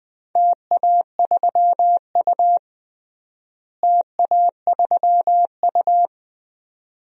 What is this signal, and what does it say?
Morse code, heard as a keyed tone, about 20 wpm, 700 Hz. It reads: TA3U TA3U
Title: DXCC-100 Call Signs - 4 Characters - Repeated 2x - 20wpm